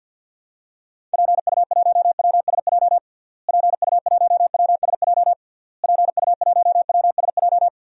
Morse code, heard as a keyed tone, about 52 wpm, 700 Hz. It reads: PU1WSJ PU1WSJ PU1WSJ
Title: DXCC-100 Call Signs - 6 Characters - Repeated 3x - 50wpm